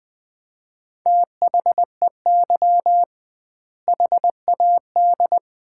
Morse code, heard as a keyed tone, about 20 wpm, 700 Hz. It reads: THEY HAD
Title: 2x Word-Spacing - Sets of 2 Words - 20wpm